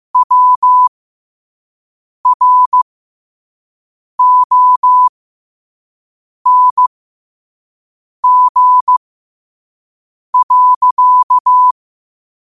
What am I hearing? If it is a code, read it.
Morse code, heard as a keyed tone, about 15 wpm, spaced as for 5 wpm, 1000 Hz. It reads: WRONG.
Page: learn-to-code